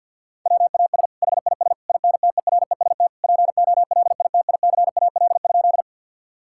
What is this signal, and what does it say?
Morse code, heard as a keyed tone, about 50 wpm, 700 Hz. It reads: WAS HIS INTEREST POLITICAL?
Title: Sentences from Top 1000 Words - Review Part 5 - 50wpm